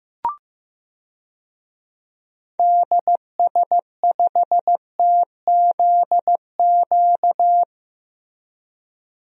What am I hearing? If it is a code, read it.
Morse code, heard as a keyed tone, about 15 wpm, 700 Hz. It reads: DS5TZQ
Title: DXCC-100 Call Signs - 6 Characters - Repeated 3x - 15wpm